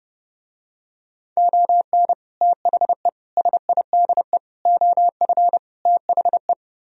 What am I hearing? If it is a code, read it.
Morse code, heard as a keyed tone, about 30 wpm, 700 Hz. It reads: ON THE SIDE OF THE